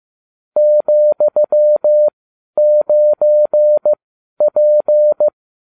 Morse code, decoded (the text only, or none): ,9P